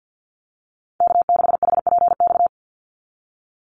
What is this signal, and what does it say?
Morse code, heard as a keyed tone, about 50 wpm, 700 Hz. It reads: K6HPX
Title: NA SOTA Chaser Callsigns 50wpm